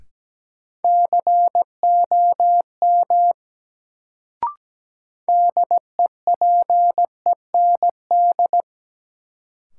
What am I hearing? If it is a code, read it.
Morse code, heard as a keyed tone, about 17 wpm, 700 Hz. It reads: COM DEPEND
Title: Sentences from Top 1300 Words - New Word Drill - 17wpm